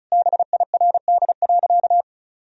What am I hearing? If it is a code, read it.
Morse code, heard as a keyed tone, about 34 wpm, 700 Hz. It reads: BIRD.
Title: Sentences from Top 1000 Words - Review Part 2 - 35wpm